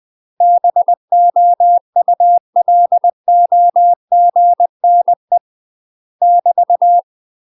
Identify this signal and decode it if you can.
Morse code, heard as a keyed tone, about 20 wpm, 700 Hz. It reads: BOULOGNE <BT>
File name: UK MASTER QSOs TRAINING / 20WPM / UK MASTER QSOs 0059